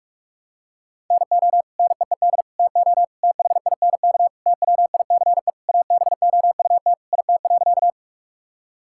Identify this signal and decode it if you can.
Morse code, heard as a keyed tone, about 46 wpm, 700 Hz. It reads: NO NEED TO THINK TWICE ABOUT IT.
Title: Sentences from Top 1600 Words 45wpm